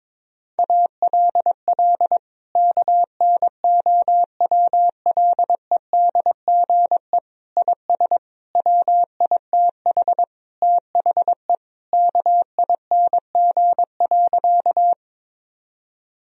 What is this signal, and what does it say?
Morse code, heard as a keyed tone, about 22 wpm, 700 Hz. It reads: ALL KNOWLEDGE IS WITH THE KING.
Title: Sentences from Top 1000 Words - Review Part 3 - 22wpm